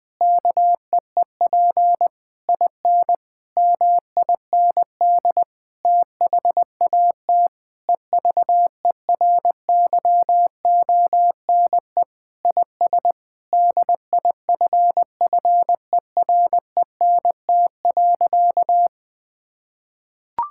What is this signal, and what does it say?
Morse code, heard as a keyed tone, about 21 wpm, 700 Hz. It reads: KEEP IN MIND THAT EVERYONE IS DIFFERENT.
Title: Sentences with 8-Word Phrases - Spoken Beginning - 20wpm